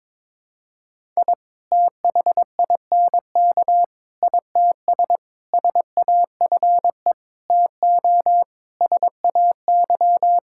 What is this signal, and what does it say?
Morse code, heard as a keyed tone, about 23 wpm, 700 Hz. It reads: I THINK ITS SAFE TO SAY
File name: practice / 22wpm / Sets of 7 Words - 22wpm